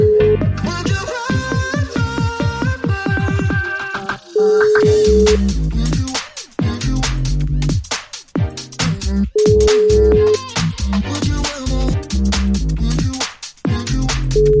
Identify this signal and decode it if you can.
Ringback tone (Europe, 425 Hz), with noise.